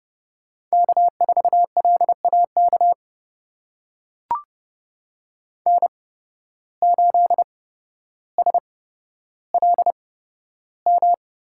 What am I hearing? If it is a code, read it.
Morse code, heard as a keyed tone, about 30 wpm, spaced as for 13 wpm, 700 Hz. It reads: K4LAK N8SLM
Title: USA Call Signs - 5 Characters - 8x Character Spacing - 30wpm